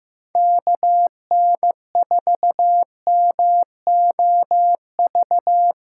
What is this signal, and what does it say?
Morse code, heard as a keyed tone, about 15 wpm, 700 Hz. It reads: KN4MOV
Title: USA Callsign DE Callsign - 6 Characters - 15wpm